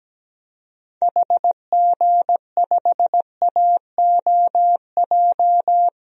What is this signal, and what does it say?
Morse code, heard as a keyed tone, about 17 wpm, 700 Hz. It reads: HG5AOJ
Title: DXCC-100 Call Signs - 6 Characters - 17wpm